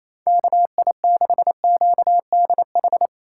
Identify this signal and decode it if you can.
Morse code, heard as a keyed tone, about 27 wpm, 700 Hz. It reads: KI6QDH
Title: K1USN SST QSO Practice - 28wpm